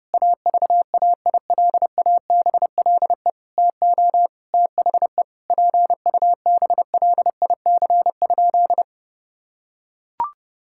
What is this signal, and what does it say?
Morse code, heard as a keyed tone, about 32 wpm, 700 Hz. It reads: AVAILABLE TO THE PUBLIC?
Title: Sentences from Top 1000 Words - Review Part 5 - 30wpm